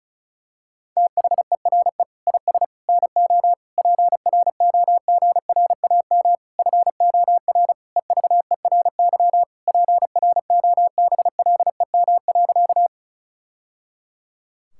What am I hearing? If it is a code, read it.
Morse code, heard as a keyed tone, about 36 wpm, 700 Hz. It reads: THERE IS NO PROGRAM FOR EVERY PROBLEM.
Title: Sentences from Top 1000 Words - Review Part 5 - 35wpm